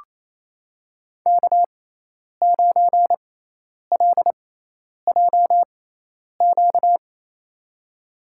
Morse code, heard as a keyed tone, about 28 wpm, spaced as for 10 wpm, 700 Hz. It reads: K9LJQ
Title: USA Call Signs - 6x Character Spacing - 28wpm